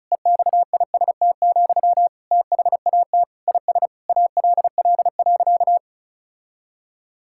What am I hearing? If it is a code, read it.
Morse code, heard as a keyed tone, about 36 wpm, 700 Hz. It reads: EXIST, THAT IS ALL.
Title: Sentences from Top 1000 Words - Review Part 5 - 35wpm